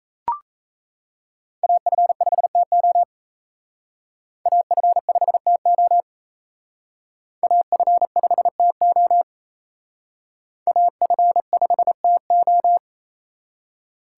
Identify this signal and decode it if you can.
Morse code, heard as a keyed tone, about 35 wpm, 700 Hz. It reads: AF5TO AF5TO AF5TO AF5TO
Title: Speed Racing - USA 2x2 Call Signs 28wpm